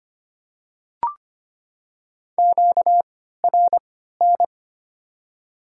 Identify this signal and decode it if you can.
Morse code, heard as a keyed tone, about 25 wpm, spaced as for 14 wpm, 690 Hz. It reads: QRN
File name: Top 90 QSO Elements - 3x Character Spacing - 25wpm